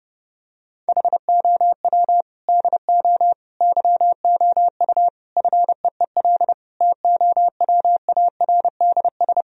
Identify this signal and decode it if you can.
Morse code, heard as a keyed tone, about 30 wpm, 700 Hz. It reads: HOW DO YOU FEEL TOWARDS
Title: Sentences from Top 1000 Words - Review Part 2 - 30wpm